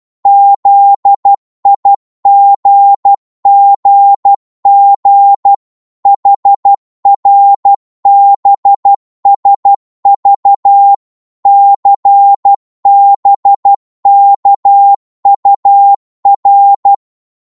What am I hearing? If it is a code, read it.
Morse code, heard as a keyed tone, about 12 wpm, 800 Hz. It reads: ZIGGG HRBSV CBKUR
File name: Morse / ltr-12wpm